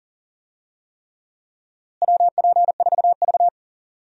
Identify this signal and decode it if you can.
Morse code, heard as a keyed tone, about 40 wpm, 700 Hz. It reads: WP4V
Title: USA Call Signs - 4 Characters - Repeated 3x - 40wpm